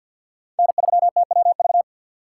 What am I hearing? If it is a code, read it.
Morse code, heard as a keyed tone, about 53 wpm, 700 Hz. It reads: N3TWV
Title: USA Call Signs - 5 Characters - 8x Character Spacing - 50wpm